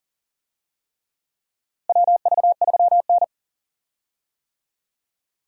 Morse code, heard as a keyed tone, about 40 wpm, 700 Hz. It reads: WV3N
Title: USA Call Signs - 4 Characters - Repeated 3x - 40wpm